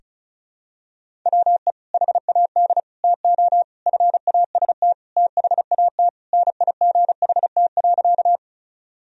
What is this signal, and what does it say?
Morse code, heard as a keyed tone, about 36 wpm, 700 Hz. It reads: WE HAD TO FAST THAT NIGHT.
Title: Sentences from Top 1000 Words - Review Part 2 - 35wpm